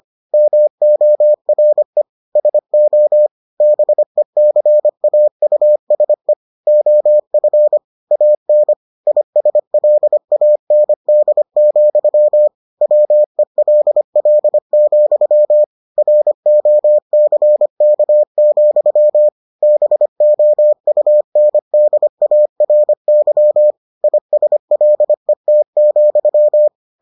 Morse code, heard as a keyed tone, about 25 wpm, 600 Hz. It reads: MORE SO BECAUSE OF AN ISLAND, WELL, ROCK, BOUNDARY ISLET,